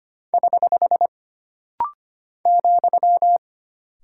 Morse code, heard as a keyed tone, about 25 wpm, 700 Hz. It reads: <HH> ,
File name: Single Punctuation-Prosign - Rapid-Fire - 25wpm